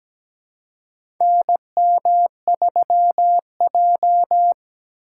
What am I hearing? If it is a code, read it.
Morse code, heard as a keyed tone, about 17 wpm, 700 Hz. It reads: NM3J
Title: USA Call Signs - 6x Character Spacing - 17wpm